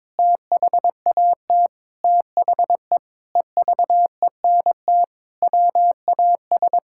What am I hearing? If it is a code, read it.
Morse code, heard as a keyed tone, about 22 wpm, 700 Hz. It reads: THAT THE EVENT WAS
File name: practice / 22wpm / Sentences with 8-Word Phrases - 22wpm